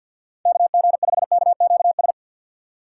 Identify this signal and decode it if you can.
Morse code, heard as a keyed tone, about 53 wpm, 700 Hz. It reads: KG5KQS
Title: USA Call Signs - 6 Characters - 4x Character Spacing - 50wpm